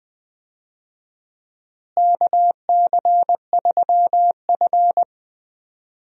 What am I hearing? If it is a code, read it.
Morse code, heard as a keyed tone, about 20 wpm, 690 Hz. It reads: KC3F
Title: USA Call Signs - 4 Characters - Repeated 3x - 20wpm